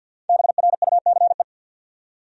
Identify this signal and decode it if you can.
Morse code, heard as a keyed tone, about 46 wpm, 700 Hz. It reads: BRUCE
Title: USA Common Names - 50wpm